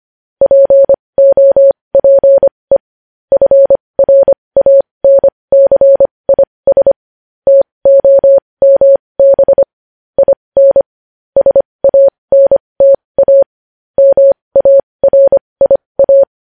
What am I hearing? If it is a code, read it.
Morse code, heard as a keyed tone, about 25 wpm, spaced as for 20 wpm, 550 Hz. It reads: POPE FRANCIS TOMB IN SANTA MARIA
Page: news headlines in morse code at 20 wpm